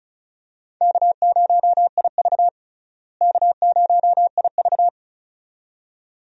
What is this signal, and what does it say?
Morse code, heard as a keyed tone, about 34 wpm, 700 Hz. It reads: K0IV K0IV
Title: USA Call Signs - 4 Characters - Repeated 2x - 35wpm